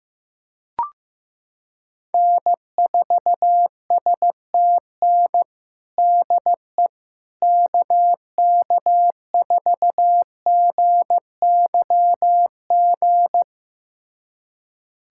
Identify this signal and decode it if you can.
Morse code, heard as a keyed tone, about 15 wpm, 700 Hz. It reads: N4STN DE KK4GYG
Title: USA Callsign DE Callsign 15wpm